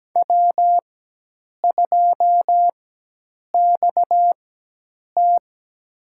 Morse code, heard as a keyed tone, about 17 wpm, spaced as for 8 wpm, 700 Hz. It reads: W2XT
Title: USA Call Signs - 4 Characters - 4x Character Spacing - 17wpm